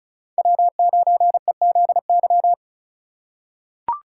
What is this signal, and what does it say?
Morse code, heard as a keyed tone, about 34 wpm, 700 Hz. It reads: W9EZY
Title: USA Call Signs - 5 Characters - 3x Character Spacing - 35wpm